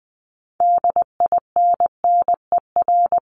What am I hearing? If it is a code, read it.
Morse code, heard as a keyed tone, about 20 wpm, 700 Hz. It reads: DINNER
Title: Top 1751-2000 Words 20wpm